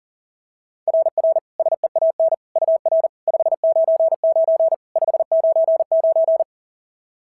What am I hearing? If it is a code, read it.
Morse code, heard as a keyed tone, about 34 wpm, 660 Hz. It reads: RR SEAN UR 599 599
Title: SOTA QSO Practice 40wpm